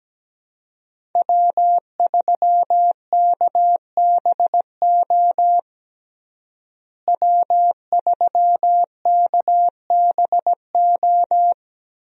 Morse code, heard as a keyed tone, about 17 wpm, 700 Hz. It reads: W3KBO W3KBO
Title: USA Call Signs - 5 Characters - Repeated 3x - 17wpm